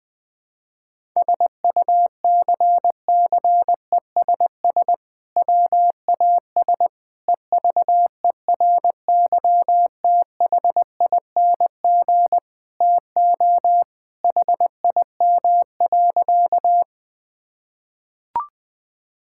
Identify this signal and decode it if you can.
Morse code, heard as a keyed tone, about 20 wpm, 700 Hz. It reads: SUCCESS WAS EVERYTHING TO HIM.
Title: Sentences from Top 1000 Words - Review Part 6 - 20wpm